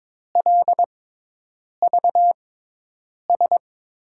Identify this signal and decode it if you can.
Morse code, heard as a keyed tone, about 22 wpm, spaced as for 8 wpm, 700 Hz. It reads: LVS
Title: USA Call Signs - 6x Character Spacing - 22wpm